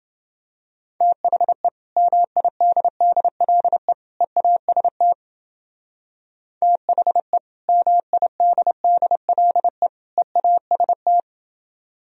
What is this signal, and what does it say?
Morse code, heard as a keyed tone, about 29 wpm, 700 Hz. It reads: THE MIDDLE EAST THE MIDDLE EAST